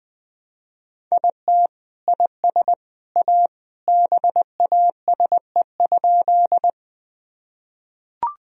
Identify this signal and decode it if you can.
Morse code, heard as a keyed tone, about 20 wpm, 700 Hz. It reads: IT IS A BASE?
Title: Sentences from Top 1000 Words - Review Part 1 - 20wpm